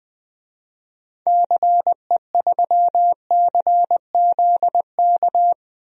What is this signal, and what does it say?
Morse code, heard as a keyed tone, about 20 wpm, 700 Hz. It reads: CE3CZK
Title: DXCC-100 Call Signs - 6 Characters - 20wpm